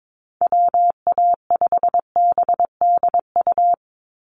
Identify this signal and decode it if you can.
Morse code, heard as a keyed tone, about 22 wpm, 700 Hz. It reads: WA5BDU
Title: Call Signs Repeated 3x - 22wpm